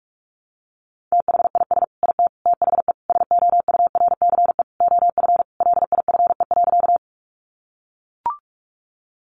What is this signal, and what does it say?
Morse code, heard as a keyed tone, about 44 wpm, 700 Hz. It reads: THIS IT THE SOURCE OF LIFE.
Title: Sentences from Top 800 Words 45wpm